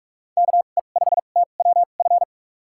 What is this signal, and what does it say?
Morse code, heard as a keyed tone, about 42 wpm, spaced as for 31 wpm, 700 Hz. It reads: KE5TWF
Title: USA Call Signs - 6 Characters - 2x Character Spacing - 45wpm